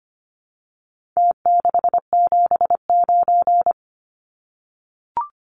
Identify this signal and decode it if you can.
Morse code, heard as a keyed tone, about 25 wpm, 700 Hz. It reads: T679
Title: Numbers - Contest Style - 4-Digit 25wpm